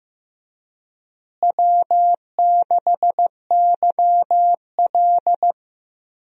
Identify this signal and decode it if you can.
Morse code, heard as a keyed tone, about 16 wpm, 700 Hz. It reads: W6YL
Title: USA Call Signs - 4 Characters - Repeated 3x - 15wpm